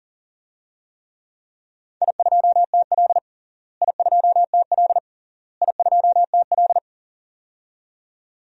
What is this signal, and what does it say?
Morse code, heard as a keyed tone, about 40 wpm, 700 Hz. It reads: I2TL I2TL I2TL
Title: DXCC-100 Call Signs - 4 Characters - Repeated 3x - 40wpm